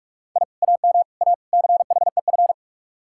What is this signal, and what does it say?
Morse code, heard as a keyed tone, about 57 wpm, 700 Hz. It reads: I AM A CHEF